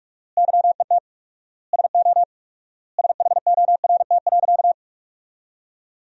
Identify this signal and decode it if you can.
Morse code, heard as a keyed tone, about 42 wpm, 700 Hz. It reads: YET SO SHORT.
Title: 4x Word-Spacing - Sentences from Top 500 Words - 45wpm